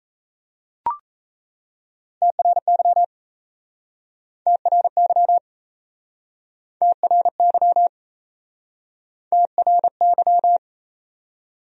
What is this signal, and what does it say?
Morse code, heard as a keyed tone, about 38 wpm, 700 Hz. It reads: TRY TRY TRY TRY